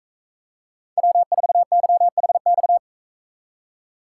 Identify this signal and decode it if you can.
Morse code, heard as a keyed tone, about 41 wpm, 700 Hz. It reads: W4YHX